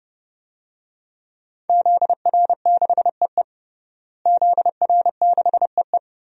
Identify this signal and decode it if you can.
Morse code, heard as a keyed tone, about 32 wpm, 700 Hz. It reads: ZR6EE ZR6EE
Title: DXCC-100 Call Signs - 5 Characters - Repeated 2x - 30wpm